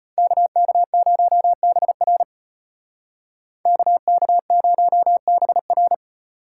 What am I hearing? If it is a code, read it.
Morse code, heard as a keyed tone, about 35 wpm, 700 Hz. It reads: KK0BR KK0BR